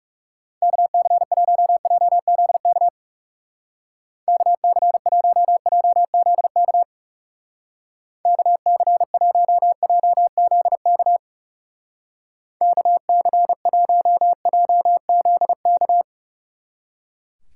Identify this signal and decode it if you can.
Morse code, heard as a keyed tone, about 35 wpm, 700 Hz. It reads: KC1JZK KC1JZK KC1JZK KC1JZK
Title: Speed Racing - USA Call Signs 30wpm